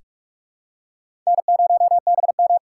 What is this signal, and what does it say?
Morse code, heard as a keyed tone, about 47 wpm, 700 Hz. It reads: N0BM
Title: USA Call Signs - 4 Characters - 2x Character Spacing - 45wpm